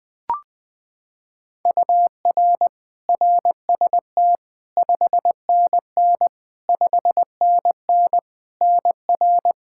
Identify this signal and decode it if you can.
Morse code, heard as a keyed tone, about 20 wpm, 700 Hz. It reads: UR RST 5NN 5NN NR
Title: Practice QSO - Exchange 1 - Beginning - 20wpm